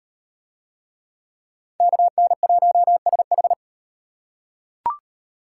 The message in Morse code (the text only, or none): KN1SH